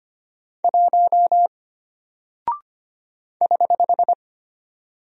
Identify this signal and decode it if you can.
Morse code, heard as a keyed tone, about 25 wpm, 700 Hz. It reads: 1 <HH>
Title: Single Character (Letter:Number:Punctuation:Prosign) Alt 25wpm